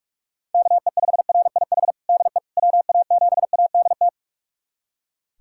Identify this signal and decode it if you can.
Morse code, heard as a keyed tone, about 41 wpm, 700 Hz. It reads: KE5RIH DE WA7ADT
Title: USA Callsign DE Callsign - 6 Characters - 45wpm